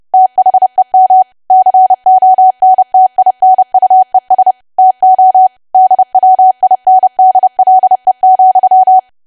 Morse code, heard as a keyed tone, about 30 wpm, 750 Hz. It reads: THEM CONTINUES TO DWINDLE,
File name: cwa / W1AW-Level3 / 30WPM